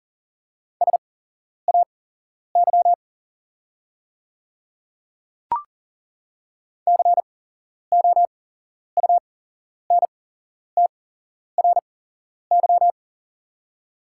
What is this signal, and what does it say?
Morse code, heard as a keyed tone, about 40 wpm, spaced as for 11 wpm, 690 Hz. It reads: SAY COUNTRY
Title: 8x Character Spacing - Top 500 Words - 40wpm